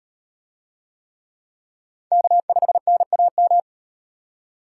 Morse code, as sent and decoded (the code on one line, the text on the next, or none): -.- ..... -. .- --
K5NAM